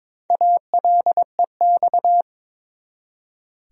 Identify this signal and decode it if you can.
Morse code, heard as a keyed tone, about 22 wpm, 700 Hz. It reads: ALEX
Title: USA Common Names - 22wpm